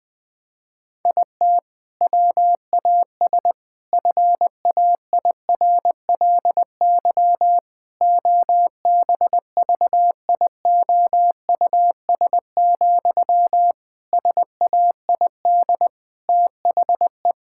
Morse code, heard as a keyed tone, about 19 wpm, 700 Hz. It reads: IT WAS FAIRLY OBVIOUS, SAID THE
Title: Sentences from Top 1600 Words 20wpm